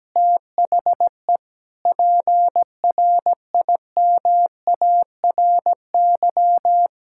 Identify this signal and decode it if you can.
Morse code, heard as a keyed tone, about 17 wpm, 700 Hz. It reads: THE PRIMARY
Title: Sentences from Top 1000 Words - Review Part 5 - 17wpm